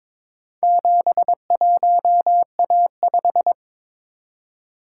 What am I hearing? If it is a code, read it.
Morse code, heard as a keyed tone, about 22 wpm, 700 Hz. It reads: 71A5
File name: practice / 22wpm / Four Letter-Number Combinations - 22wpm